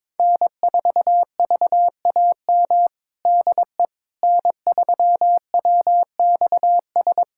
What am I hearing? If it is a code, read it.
Morse code, heard as a keyed tone, about 21 wpm, 700 Hz. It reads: N4VAM DE N3WXS
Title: USA Callsign DE Callsign - 5 Characters - 22wpm